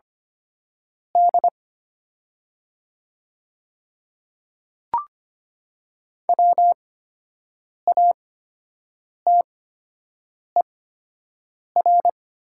Morse code, heard as a keyed tone, about 25 wpm, spaced as for 7 wpm, 700 Hz. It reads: D WATER
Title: 8x Character Spacing - Top 500 Words - 25wpm